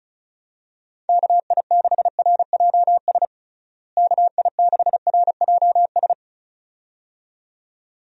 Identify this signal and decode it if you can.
Morse code, heard as a keyed tone, about 36 wpm, 700 Hz. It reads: KI6RJS KI6RJS
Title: USA Call Signs - 6 Characters - Repeated 2x - 35wpm